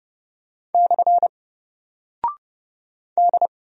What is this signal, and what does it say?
Morse code, heard as a keyed tone, about 30 wpm, 700 Hz. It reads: / D
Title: Single Character (Letter:Number:Punctuation:Prosign) 30wpm